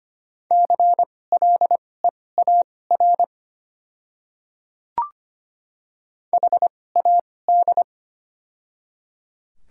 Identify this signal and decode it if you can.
Morse code, heard as a keyed tone, about 25 wpm, spaced as for 18 wpm, 700 Hz. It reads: CLEAR HAD
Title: 2x Character Spacing - Top 500 Words - 25wpm